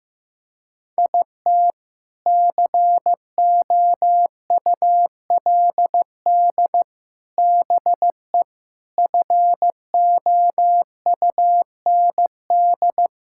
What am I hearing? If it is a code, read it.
Morse code, heard as a keyed tone, about 15 wpm, 700 Hz. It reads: IT COULD BE FOUND
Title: Sentences from Top 1000 Words - Review Part 5 - 15wpm